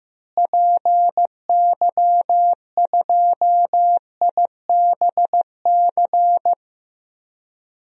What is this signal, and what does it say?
Morse code, heard as a keyed tone, about 15 wpm, 700 Hz. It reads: PY2IBC
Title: DXCC-100 Call Signs - 6 Characters - Repeated 3x - 15wpm